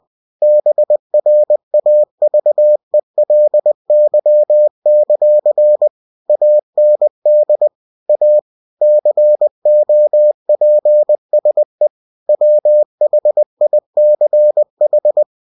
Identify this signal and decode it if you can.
Morse code, heard as a keyed tone, about 20 wpm, 600 Hz. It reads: BRAVELY; AND A COPSE WHICH
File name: cw-books / winnie-the-pooh / winnie-20wpm-page-0028